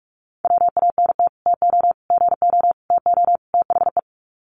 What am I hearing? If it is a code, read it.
Morse code, heard as a keyed tone, about 44 wpm, 700 Hz. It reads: WANT TO GO TO THE
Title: Sets of 5 Words Encore2 45wpm